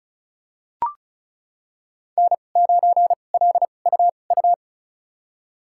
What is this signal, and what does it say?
Morse code, heard as a keyed tone, about 36 wpm, spaced as for 25 wpm, 700 Hz. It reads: N9LUU
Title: USA Call Signs - 5 Characters - 2x Character Spacing - 35wpm